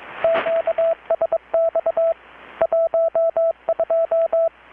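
Morse code, heard as a keyed tone, about 22 wpm, 660 Hz, with radio noise. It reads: QSX 12
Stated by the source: Morse from KPH's Night of Nights XX maritime radiotelegraphy event, received in Chiapas, Mexico about 3,800 km away on a Kenwood R-600 with a grounded 10 m long-wire antenna.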